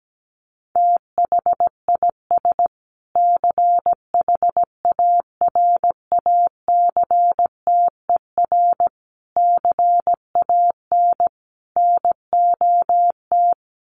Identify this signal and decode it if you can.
Morse code, heard as a keyed tone, about 17 wpm, 700 Hz. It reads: THIS CHARACTER CAN NOT
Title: Sets of 5 Words Encore2 17wpm